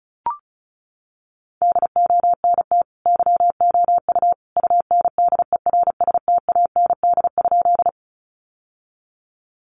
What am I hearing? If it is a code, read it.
Morse code, heard as a keyed tone, about 34 wpm, 700 Hz. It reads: DONT YOU UNDERSTAND?
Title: Sentences from Top 600 Words 35wpm